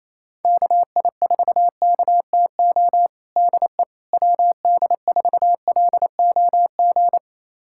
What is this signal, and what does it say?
Morse code, heard as a keyed tone, about 27 wpm, 700 Hz. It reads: KI4KTO DE WD4LOG
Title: USA Callsign DE Callsign - 6 Characters - 28wpm